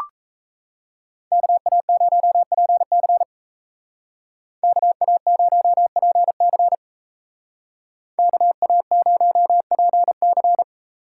Morse code, heard as a keyed tone, about 38 wpm, 700 Hz. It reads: KA0PC KA0PC KA0PC